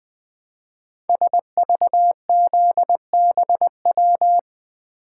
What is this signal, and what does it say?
Morse code, heard as a keyed tone, about 20 wpm, 700 Hz. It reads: SVZBW
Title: Code Groups - 5 Letters - 20wpm